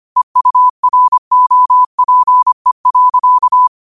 Morse code, heard as a keyed tone, about 25 wpm, 1000 Hz. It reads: EUROPE.